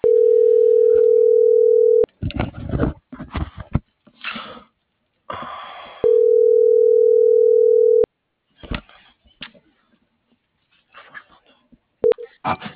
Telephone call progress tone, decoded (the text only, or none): ringback tone